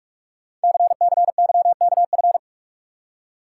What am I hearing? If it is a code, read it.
Morse code, heard as a keyed tone, about 44 wpm, 700 Hz. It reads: CCYKF
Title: Code Groups - 5 Letters - 45wpm